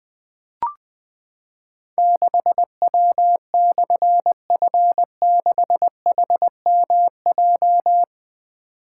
Morse code, heard as a keyed tone, about 20 wpm, 700 Hz. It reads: 6W/F6HMJ
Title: Difficult Call Signs 20wpm